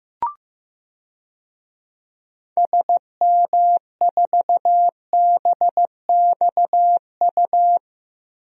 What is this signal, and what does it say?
Morse code, heard as a keyed tone, about 15 wpm, 700 Hz. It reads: SM4BXU
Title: DXCC-100 Call Signs - 6 Characters - Repeated 3x - 15wpm